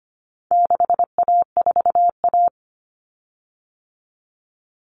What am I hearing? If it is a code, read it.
Morse code, heard as a keyed tone, about 25 wpm, 700 Hz. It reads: 6A4A
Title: Common Cut Numbers - 4-Digit 25wpm